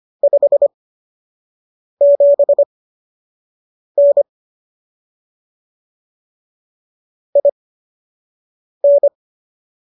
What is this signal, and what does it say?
Morse code, heard as a keyed tone, about 25 wpm, spaced as for 6 wpm, 570 Hz. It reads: 57N IN
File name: qso012 06